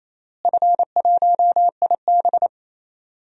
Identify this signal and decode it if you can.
Morse code, heard as a keyed tone, about 28 wpm, 700 Hz. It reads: F1IB
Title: DXCC-100 Call Signs - 4 Characters - Repeated 3x - 28wpm